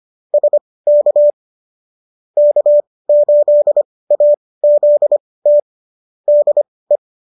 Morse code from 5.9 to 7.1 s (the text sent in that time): DE